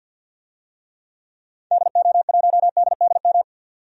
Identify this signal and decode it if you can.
Morse code, heard as a keyed tone, about 47 wpm, 700 Hz. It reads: DO1DDM
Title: DXCC-100 Call Signs - 6 Characters - 50wpm